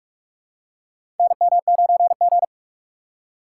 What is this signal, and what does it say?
Morse code, heard as a keyed tone, about 46 wpm, 700 Hz. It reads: NM9G
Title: USA Call Signs - 4 Characters - 3x Character Spacing - 45wpm